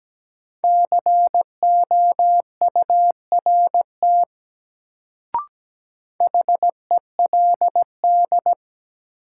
Morse code, heard as a keyed tone, about 17 wpm, 700 Hz. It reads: COURT HELD
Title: Sentences from Top 1600 Words - New Word Drill Alt - 17wpm